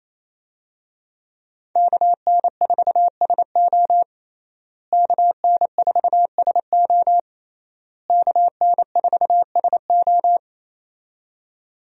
Morse code, heard as a keyed tone, about 28 wpm, 700 Hz. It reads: KN4SO KN4SO KN4SO
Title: USA Call Signs - 5 Characters - Repeated 3x - 28wpm